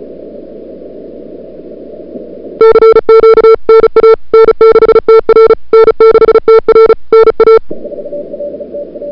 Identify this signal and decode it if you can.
Morse code, heard as a keyed tone, about 34 wpm, 440 Hz. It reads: CQ NA N6TR N6TR NA
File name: CWSprint / February 2007 CW Sprint / 01-2007 CW Sprint Hour1